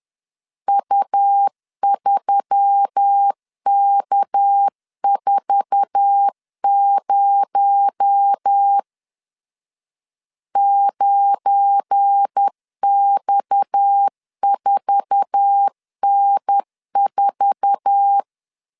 Morse code, heard as a keyed tone, about 10 wpm, 800 Hz. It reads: U3K40 9X4N4
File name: LEZIONI / 8 YJ45 / 8 YJ4 10wpm 2633